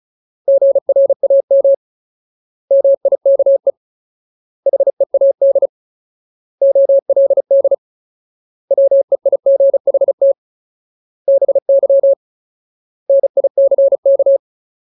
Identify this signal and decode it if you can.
Morse code, heard as a keyed tone, about 34 wpm, 550 Hz. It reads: GRAM MIKE HEAD OLD WEIGHT BY NICK
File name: Ultimate Word List / 35WPM Ultimate